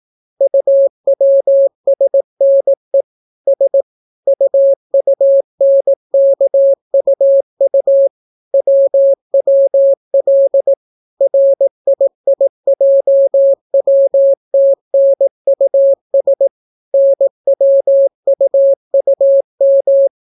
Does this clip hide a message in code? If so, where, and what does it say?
Yes: Morse code from 0.4 to 20.1 s, heard as a keyed tone, about 18 wpm, 550 Hz. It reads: UWSNE S UUNKUU WWL RIIJWTNUS NWUUM